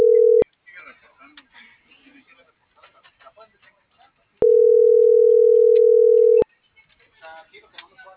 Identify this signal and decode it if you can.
Ringback tone (North America), 440 + 480 Hz, with speech.
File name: CENTRAL AGUILERA